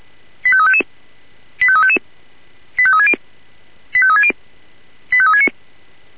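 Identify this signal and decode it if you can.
Five-tone selective call (ZVEI), heard as quick runs of five tones, tones of about 70 ms. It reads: 85355 85355 85360 85360 85369
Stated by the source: ZVEI five-tone selective call received by SDR.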